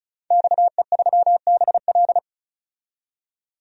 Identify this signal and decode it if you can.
Morse code, heard as a keyed tone, about 36 wpm, 700 Hz. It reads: XE3BL
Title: DXCC-100 Call Signs - 5 Characters - 35wpm